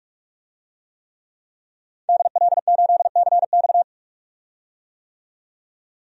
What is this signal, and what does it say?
Morse code, heard as a keyed tone, about 46 wpm, 700 Hz. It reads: DL8CX
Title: DXCC-100 Call Signs - 5 Characters - 45wpm